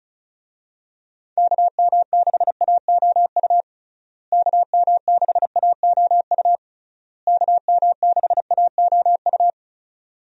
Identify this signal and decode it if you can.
Morse code, heard as a keyed tone, about 36 wpm, 700 Hz. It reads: KM6AOU KM6AOU KM6AOU
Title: USA Call Signs - 6 Characters - Repeated 3x - 35wpm